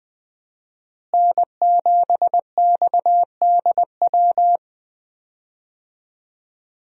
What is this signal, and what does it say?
Morse code, heard as a keyed tone, about 20 wpm, 700 Hz. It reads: N7XDW
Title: USA Call Signs - 5 Characters - Repeated 3x - 20wpm